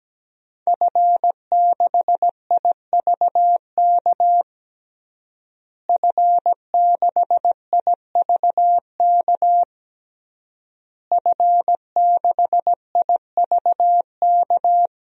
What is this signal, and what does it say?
Morse code, heard as a keyed tone, about 17 wpm, 700 Hz. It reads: F6IVK F6IVK F6IVK
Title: DXCC-100 Call Signs - 5 Characters - Repeated 3x - 17wpm